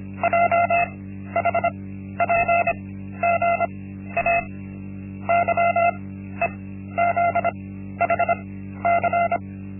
Morse code, heard as a keyed tone, about 25 wpm, spaced as for 13 wpm, 690 Hz. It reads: JHPGA YEZHC
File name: Linksmakalnis-1